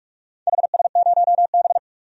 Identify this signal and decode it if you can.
Morse code, heard as a keyed tone, about 44 wpm, 700 Hz. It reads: HS0B
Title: DXCC-100 Call Signs - 4 Characters - 45wpm